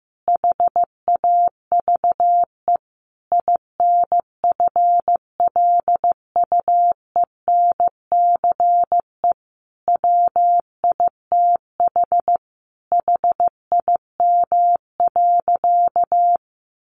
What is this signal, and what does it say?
Morse code, heard as a keyed tone, about 15 wpm, 700 Hz. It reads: HAVE INFLUENCE WITH HIM.
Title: Sentences from Top 1100 Words 15wpm